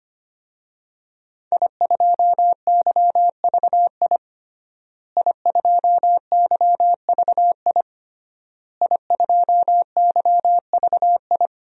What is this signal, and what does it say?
Morse code, heard as a keyed tone, about 25 wpm, 700 Hz. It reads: I2YVI I2YVI I2YVI
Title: DXCC-100 Call Signs - 5 Characters - Repeated 3x - 25wpm